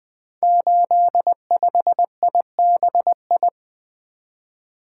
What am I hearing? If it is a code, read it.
Morse code, heard as a keyed tone, about 20 wpm, 700 Hz. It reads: 85IBI
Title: Code Groups - 5 Letter:Numbers - 20wpm